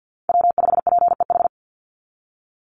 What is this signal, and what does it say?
Morse code, heard as a keyed tone, about 50 wpm, 700 Hz. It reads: W5PEH
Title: Call Signs Repeated 2x - 50wpm